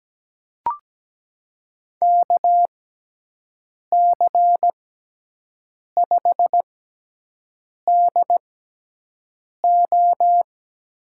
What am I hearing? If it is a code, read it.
Morse code, heard as a keyed tone, about 17 wpm, spaced as for 6 wpm, 700 Hz. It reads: KC5DO
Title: USA Call Signs - 6 Characters - 6x Character Spacing - 17wpm